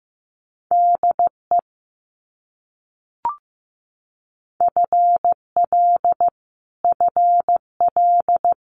Morse code, heard as a keyed tone, about 15 wpm, 700 Hz. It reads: DE FL FL
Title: US States Repeated 3x 15wpm